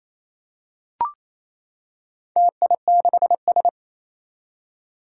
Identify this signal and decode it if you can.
Morse code, heard as a keyed tone, about 28 wpm, 700 Hz. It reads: TI6S
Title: Four Challenging Letter-Number Combinations 28wpm